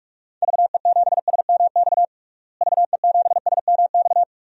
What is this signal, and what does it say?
Morse code, heard as a keyed tone, about 46 wpm, 700 Hz. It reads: VE7SMX VE7SMX
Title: DXCC-100 Call Signs - 6 Characters - Repeated 2x - 45wpm